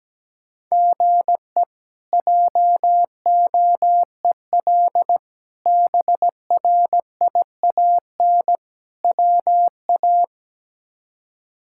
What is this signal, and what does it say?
Morse code, heard as a keyed tone, about 17 wpm, 700 Hz. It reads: GE JOEL BRIAN WA
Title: K1USN SST QSO Practice - 17wpm